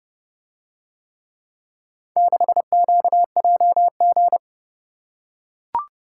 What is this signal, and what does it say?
Morse code, heard as a keyed tone, about 31 wpm, 700 Hz. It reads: 6QJG